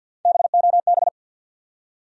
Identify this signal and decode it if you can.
Morse code, heard as a keyed tone, about 48 wpm, 700 Hz. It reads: BOB
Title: USA Common Names - Max 3 Letters - 50wpm